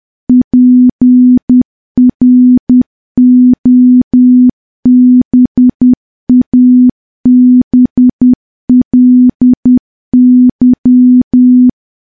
Morse code, heard as a keyed tone, about 10 wpm, 250 Hz. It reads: PROBABLY
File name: morse / morse-training-10WPM